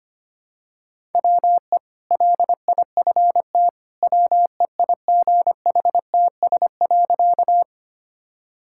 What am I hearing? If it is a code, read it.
Morse code, heard as a keyed tone, about 25 wpm, 690 Hz. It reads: WE LIFT WEIGHTS.
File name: Easy Sentences with Hints - 25wpm